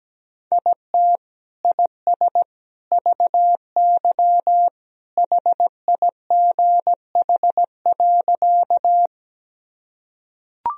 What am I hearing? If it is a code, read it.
Morse code, heard as a keyed tone, about 17 wpm, 700 Hz. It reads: IT IS VY HIGH.
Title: Sentences from Top 200 Words and Abbreviations 17wpm